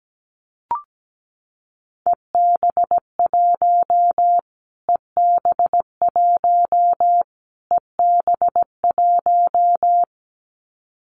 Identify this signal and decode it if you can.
Morse code, heard as a keyed tone, about 17 wpm, 700 Hz. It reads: EB1 EB1 EB1
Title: Callsign Prefixes for Top 25 DXCC - Repeated 3x - 17wpm